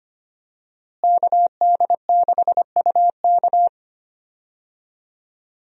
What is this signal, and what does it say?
Morse code, heard as a keyed tone, about 25 wpm, 700 Hz. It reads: KD6UK
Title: USA Call Signs - 5 Characters - Repeated 2x - 25wpm